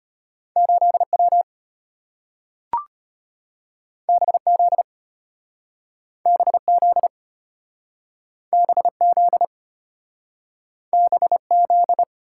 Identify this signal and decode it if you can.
Morse code, heard as a keyed tone, about 34 wpm, 700 Hz. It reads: 8W BZ BZ BZ BZ